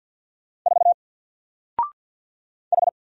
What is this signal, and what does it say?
Morse code, heard as a keyed tone, about 47 wpm, 700 Hz. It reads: 4 H
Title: Single Letter-Number 50wpm